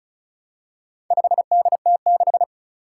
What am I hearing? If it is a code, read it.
Morse code, heard as a keyed tone, about 36 wpm, 700 Hz. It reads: 5DT6
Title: Four Challenging Letter-Number Combinations 35wpm